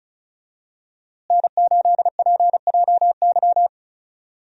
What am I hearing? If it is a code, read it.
Morse code, heard as a keyed tone, about 34 wpm, 700 Hz. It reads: N8PJY
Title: USA Call Signs - 5 Characters - Repeated 3x - 35wpm